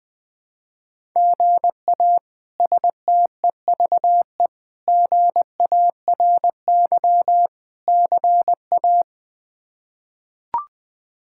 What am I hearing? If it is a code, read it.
Morse code, heard as a keyed tone, about 20 wpm, 700 Hz. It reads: GA STEVE GARY CA
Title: K1USN SST QSO Practice - 20wpm